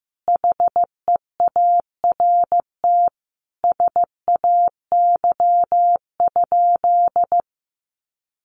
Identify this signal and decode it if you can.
Morse code, heard as a keyed tone, about 15 wpm, 700 Hz. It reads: HEART SAY?
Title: Sentences from Top 1200 Words 15wpm